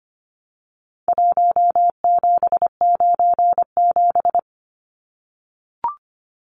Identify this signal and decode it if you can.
Morse code, heard as a keyed tone, about 25 wpm, 700 Hz. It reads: 1797
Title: Numbers - Contest Style - 4-Digit 25wpm